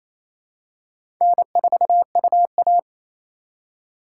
Morse code, heard as a keyed tone, about 28 wpm, 700 Hz. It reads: N4UA
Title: USA Call Signs - 4 Characters - 2x Character Spacing - 28wpm